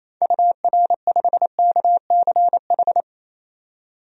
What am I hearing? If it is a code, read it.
Morse code, heard as a keyed tone, about 28 wpm, 700 Hz. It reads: UR5KCH
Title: DXCC-100 Call Signs - 6 Characters - 28wpm